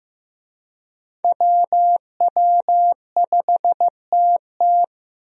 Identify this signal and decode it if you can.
Morse code, heard as a keyed tone, about 16 wpm, 700 Hz. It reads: WW5TT
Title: USA Call Signs - 5 Characters - Repeated 3x - 15wpm